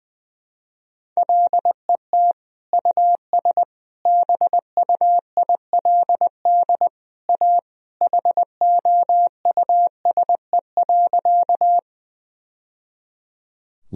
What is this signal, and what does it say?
Morse code, heard as a keyed tone, about 21 wpm, 700 Hz. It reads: LET US BUILD A HOUSE.
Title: Sentences from Top 1000 Words - Review Part 1 - 20wpm